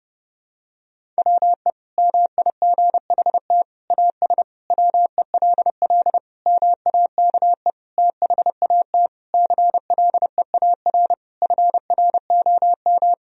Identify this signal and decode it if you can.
Morse code, heard as a keyed tone, about 32 wpm, 700 Hz. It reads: WE MIGHT AS WELL MAKE THAT CLEAR FROM
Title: Sentences from Top 1000 Words - Review Part 1 - 30wpm